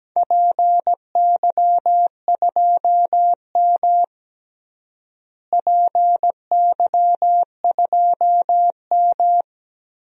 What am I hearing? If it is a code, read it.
Morse code, heard as a keyed tone, about 17 wpm, 700 Hz. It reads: PY2M PY2M
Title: DXCC-100 Call Signs - 4 Characters - Repeated 2x - 17wpm